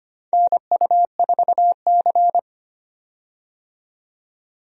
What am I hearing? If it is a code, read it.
Morse code, heard as a keyed tone, about 25 wpm, 700 Hz. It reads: NU4C
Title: USA Call Signs - 4 Characters - 25wpm